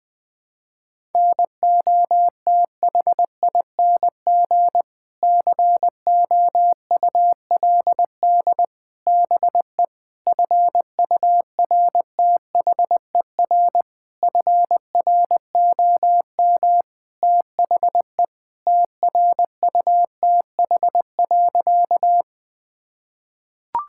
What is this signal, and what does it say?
Morse code, heard as a keyed tone, about 20 wpm, 700 Hz. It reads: NOTHING COULD BE FURTHER FROM THE TRUTH.
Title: Sentences with 6-Word Phrases - Spoken Beginning - 20wpm